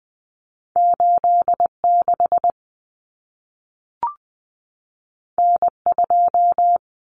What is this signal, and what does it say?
Morse code, heard as a keyed tone, about 20 wpm, 700 Hz. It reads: 86 N2
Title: Common Cut Numbers - 2-Digit 20wpm